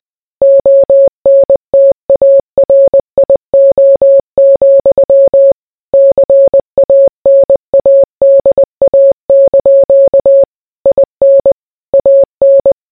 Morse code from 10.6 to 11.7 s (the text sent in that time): IN